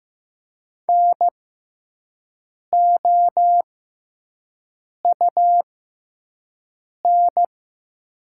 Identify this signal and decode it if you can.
Morse code, heard as a keyed tone, about 15 wpm, spaced as for 5 wpm, 700 Hz. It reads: NOUN
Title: 6x Character Spacing - Top 500 Words - 15wpm